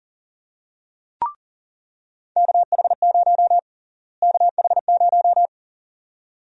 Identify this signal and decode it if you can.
Morse code, heard as a keyed tone, about 40 wpm, 700 Hz. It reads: KH0 KH0
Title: USA Call Sign Prefixes - Repeated 2x - 40wpm